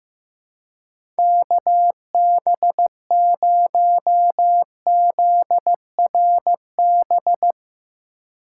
Morse code, heard as a keyed tone, about 15 wpm, 700 Hz. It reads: KB0ZRB
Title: USA Call Signs - 6 Characters - 2x Character Spacing - 15wpm